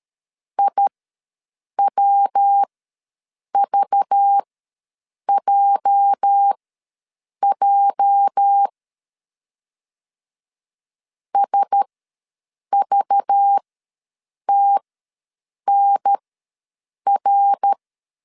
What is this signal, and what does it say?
Morse code, heard as a keyed tone, about 13 wpm, spaced as for 7 wpm, 800 Hz. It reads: IWVJJ SVTNR
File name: LEZIONI / 8 YJ45 / 8 YJ ws 12wpm 3342